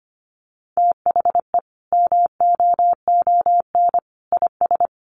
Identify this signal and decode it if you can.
Morse code, heard as a keyed tone, about 25 wpm, 700 Hz. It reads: THE MOON IS
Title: Sentences from Top 1200 Words 25wpm